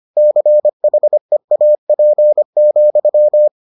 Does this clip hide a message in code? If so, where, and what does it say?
Yes: Morse code from 0.2 to 3.5 s, heard as a keyed tone, about 25 wpm, 600 Hz. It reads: CHEAP,